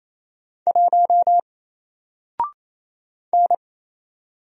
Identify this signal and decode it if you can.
Morse code, heard as a keyed tone, about 27 wpm, 700 Hz. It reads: 1 N
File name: Single-Letter-Number-28wpm